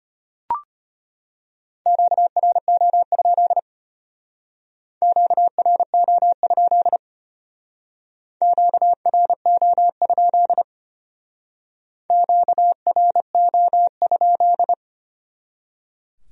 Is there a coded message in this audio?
Yes: Morse code from 1.9 to 14.7 s, heard as a keyed tone, about 33 wpm, 700 Hz. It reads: QRO? QRO? QRO? QRO?